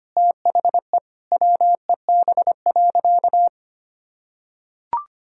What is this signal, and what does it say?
Morse code, heard as a keyed tone, about 25 wpm, 700 Hz. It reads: THE WEB.
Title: Sentences from Top 1000 Words - Review Part 4 - 25wpm